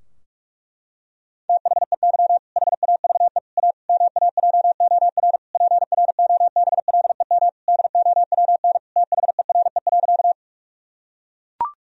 Morse code, heard as a keyed tone, about 44 wpm, 700 Hz. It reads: THEY HAVE A MAJOR PROBLEM DOWN THERE.
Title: Sentences from Top 1000 Words - Review Part 4 - 45wpm